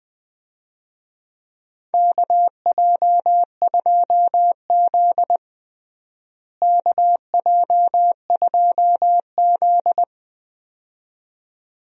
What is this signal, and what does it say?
Morse code, heard as a keyed tone, about 21 wpm, 700 Hz. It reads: KJ2Z KJ2Z
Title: USA Call Signs - 4 Characters - Repeated 2x - 20wpm